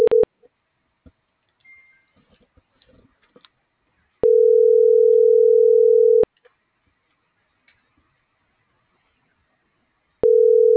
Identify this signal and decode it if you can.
Ringback tone (North America), 440 + 480 Hz.